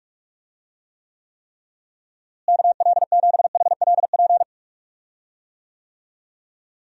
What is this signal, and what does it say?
Morse code, heard as a keyed tone, about 46 wpm, 700 Hz. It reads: KL7HLP
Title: DXCC-100 Call Signs - 6 Characters - Repeated 3x - 45wpm